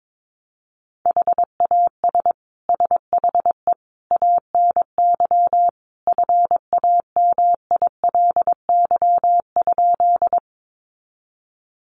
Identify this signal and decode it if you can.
Morse code, heard as a keyed tone, about 22 wpm, 700 Hz. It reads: HAS SHE ANY FAMILY?
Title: Sentences from Top 400 Words 22wpm